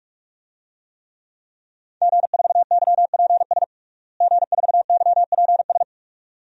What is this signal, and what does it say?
Morse code, heard as a keyed tone, about 46 wpm, 700 Hz. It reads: G4YPS G4YPS
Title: DXCC-100 Call Signs - 5 Characters - Repeated 2x - 45wpm